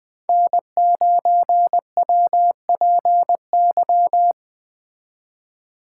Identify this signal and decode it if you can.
Morse code, heard as a keyed tone, about 20 wpm, 700 Hz. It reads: N9WPY
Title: Speed Racing - USA Call Signs 17wpm